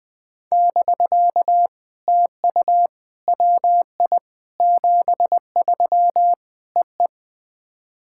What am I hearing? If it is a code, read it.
Morse code, heard as a keyed tone, about 20 wpm, 660 Hz. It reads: <BK> TU WI 73 EE
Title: POTA QSO Practice - 20wpm